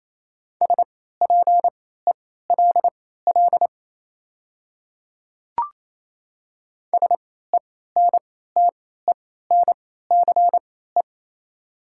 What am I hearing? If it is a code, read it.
Morse code, heard as a keyed tone, about 27 wpm, spaced as for 16 wpm, 700 Hz. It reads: SPELL SENTENCE
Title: 3x Character Spacing - Top 200 Words - 28wpm